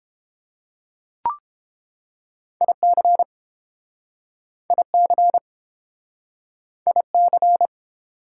Morse code, heard as a keyed tone, about 31 wpm, 700 Hz. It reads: IC IC IC